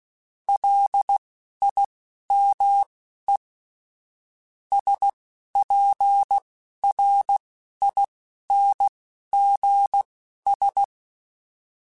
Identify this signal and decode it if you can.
Morse code, heard as a keyed tone, about 16 wpm, spaced as for 11 wpm, 800 Hz. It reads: LIME SPRINGS
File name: fromSMs / FromSMs mp3s / Cities@10